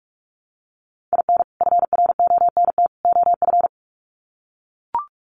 Morse code, heard as a keyed tone, about 44 wpm, 700 Hz. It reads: IN FRONT OF
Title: Sets of 3 Words 45wpm